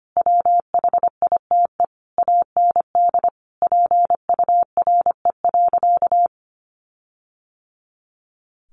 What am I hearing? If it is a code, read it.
Morse code, heard as a keyed tone, about 25 wpm, 700 Hz. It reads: WHITE AND PURE.
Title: Sentences from Top 1500 Words 25wpm